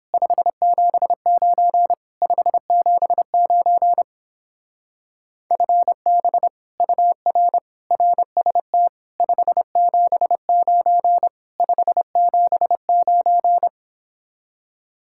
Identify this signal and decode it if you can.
Morse code, heard as a keyed tone, about 31 wpm, 700 Hz. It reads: 579 579 FB UR RST 579 579